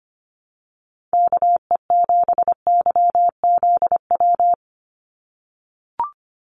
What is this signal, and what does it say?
Morse code, heard as a keyed tone, about 25 wpm, 690 Hz. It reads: KE7YZW
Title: USA Call Signs 25wpm